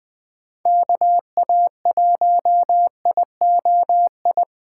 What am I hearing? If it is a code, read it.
Morse code, heard as a keyed tone, about 20 wpm, 700 Hz. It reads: KA1IOI
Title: USA Call Signs - 6 Characters - 4x Character Spacing - 20wpm